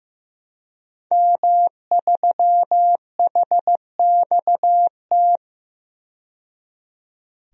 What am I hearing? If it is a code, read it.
Morse code, heard as a keyed tone, about 15 wpm, 700 Hz. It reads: M3HXT
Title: DXCC-100 Call Signs - 5 Characters - 15wpm